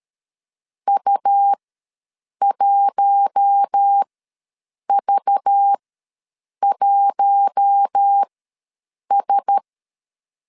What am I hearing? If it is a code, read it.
Morse code, heard as a keyed tone, about 13 wpm, spaced as for 7 wpm, 800 Hz. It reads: U1V1S
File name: LEZIONI / 5 LBVC / 5 LBV ws 12wpm 3015